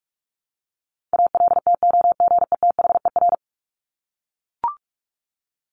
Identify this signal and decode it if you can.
Morse code, heard as a keyed tone, about 44 wpm, 700 Hz. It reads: ALTOGETHER
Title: Top 1501-1750 Words 45wpm